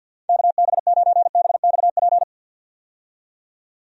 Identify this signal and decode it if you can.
Morse code, heard as a keyed tone, about 50 wpm, 700 Hz. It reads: KB9BXP
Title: USA Call Signs - 6 Characters - 50wpm